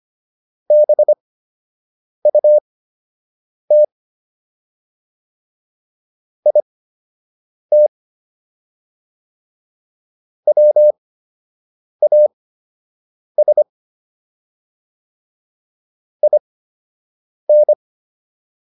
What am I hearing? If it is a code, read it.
Morse code, heard as a keyed tone, about 25 wpm, spaced as for 7 wpm, 600 Hz. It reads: BUT IT WAS IN